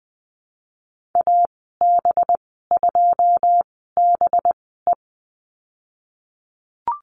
Morse code, heard as a keyed tone, about 20 wpm, spaced as for 14 wpm, 700 Hz. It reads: AB2BE
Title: 231 Lesson 37g - Letter x - Call Signs 20wpm